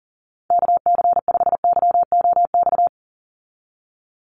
Morse code, heard as a keyed tone, about 37 wpm, 700 Hz. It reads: KC5YOX
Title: USA Call Signs 40wpm